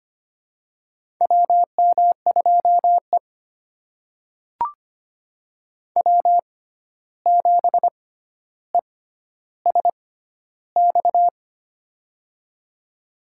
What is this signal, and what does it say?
Morse code, heard as a keyed tone, about 25 wpm, spaced as for 12 wpm, 700 Hz. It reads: WM2E W7ESX
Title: USA Call Signs - 6x Character Spacing - 25wpm